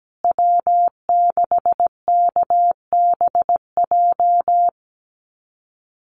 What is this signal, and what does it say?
Morse code, heard as a keyed tone, about 17 wpm, 700 Hz. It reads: W6KBJ
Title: USA Call Signs Repeated 3x 17wpm